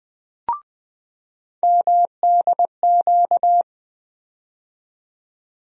Morse code, heard as a keyed tone, about 20 wpm, 700 Hz. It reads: MDQ
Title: USA Call Sign Postfixes - 3 Letters 20wpm